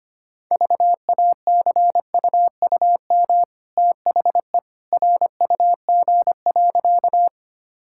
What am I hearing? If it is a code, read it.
Morse code, heard as a keyed tone, about 25 wpm, 700 Hz. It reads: VACUUM THE RUG.
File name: Easy Sentences with Hints - 25wpm